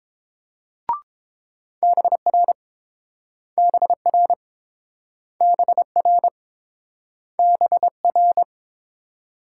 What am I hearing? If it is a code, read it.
Morse code, heard as a keyed tone, about 30 wpm, 690 Hz. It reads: BR BR BR BR